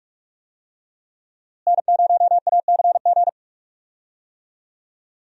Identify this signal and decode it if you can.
Morse code, heard as a keyed tone, about 47 wpm, 700 Hz. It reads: N0ACG
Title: USA Call Signs - 5 Characters - Repeated 3x - 45wpm